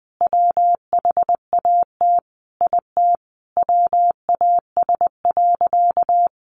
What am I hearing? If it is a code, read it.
Morse code, heard as a keyed tone, about 20 wpm, 700 Hz. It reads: WHAT IT WAS.
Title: Sentences from Top 1400 Words 20wpm